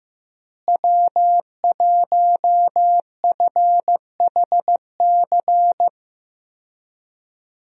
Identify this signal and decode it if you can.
Morse code, heard as a keyed tone, about 15 wpm, 700 Hz. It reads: W1FHC
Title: USA Call Signs - 5 Characters - Repeated 3x - 15wpm